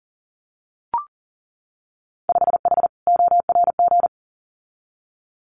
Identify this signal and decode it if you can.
Morse code, heard as a keyed tone, about 40 wpm, 700 Hz. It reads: 5H ORG
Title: Winter Field Day Exchanges Encore 40wpm